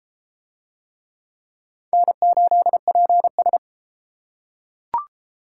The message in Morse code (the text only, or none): N8PS